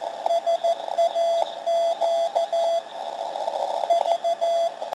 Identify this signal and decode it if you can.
Morse code, heard as a keyed tone, about 14 wpm, 680 Hz, with radio noise. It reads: SAQ V
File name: 472khz / slow%20hell / saq-natale2017